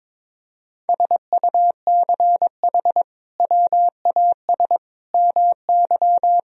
Morse code, heard as a keyed tone, about 22 wpm, 700 Hz. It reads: SUCH WAS MY
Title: Sentences from Top 1000 Words - Review Part 1 - 22wpm